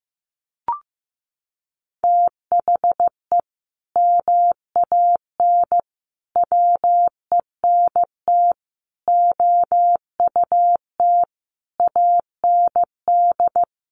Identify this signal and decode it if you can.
Morse code, heard as a keyed tone, about 15 wpm, 700 Hz. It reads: THE MAN WENT OUT AND
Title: Sentences from Top 1400 Words 15wpm